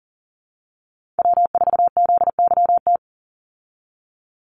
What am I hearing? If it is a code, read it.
Morse code, heard as a keyed tone, about 40 wpm, 700 Hz. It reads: W4ZYT
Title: Call Signs - 40wpm